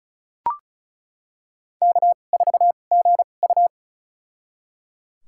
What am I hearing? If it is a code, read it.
Morse code, heard as a keyed tone, about 36 wpm, spaced as for 25 wpm, 700 Hz. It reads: K4GU
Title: USA Call Signs - 4 Characters - 2x Character Spacing - 35wpm